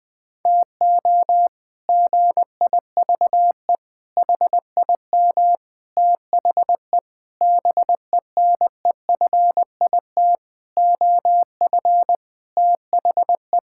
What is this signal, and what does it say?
Morse code, heard as a keyed tone, about 20 wpm, 700 Hz. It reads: TO GIVE HIM THE BENEFIT OF THE
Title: Sets of 8 Words - 20wpm